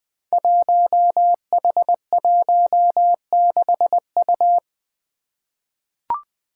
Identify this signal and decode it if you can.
Morse code, heard as a keyed tone, about 20 wpm, 700 Hz. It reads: 1H16U
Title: Five Challenging Letter-Number Combinations 20wpm